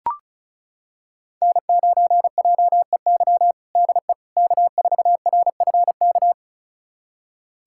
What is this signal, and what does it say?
Morse code, heard as a keyed tone, about 36 wpm, 700 Hz. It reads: N9JEY DE K4RFK
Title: USA Callsign DE Callsign - 5 Characters - 35wpm